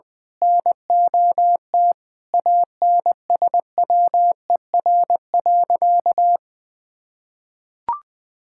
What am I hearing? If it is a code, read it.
Morse code, heard as a keyed tone, about 20 wpm, 700 Hz. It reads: NOT ANSWER.
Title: Sentences from Top 1000 Words - Review Part 6 - 20wpm